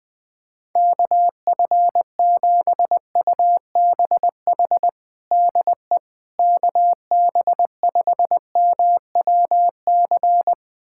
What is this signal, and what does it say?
Morse code, heard as a keyed tone, about 21 wpm, 700 Hz. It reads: KF7UBH DE KB5MWC
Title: USA Callsign DE Callsign - 6 Characters - 20wpm